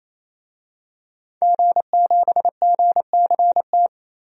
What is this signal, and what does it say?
Morse code, heard as a keyed tone, about 29 wpm, 700 Hz. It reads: G7GCT
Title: DXCC-100 Call Signs - 5 Characters - 28wpm